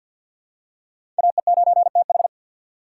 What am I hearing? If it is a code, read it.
Morse code, heard as a keyed tone, about 46 wpm, 700 Hz. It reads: AE9TH
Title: USA Call Signs - 6x Character Spacing - 50wpm